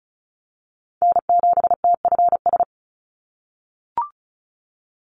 Morse code, heard as a keyed tone, about 34 wpm, 700 Hz. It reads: N7TFS
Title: USA Call Signs 35wpm